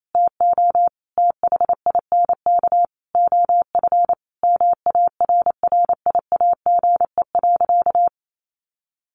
Morse code, heard as a keyed tone, about 28 wpm, 700 Hz. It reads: TO THINK OF MARRIAGE.
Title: Sentences from Top 800 Words 28wpm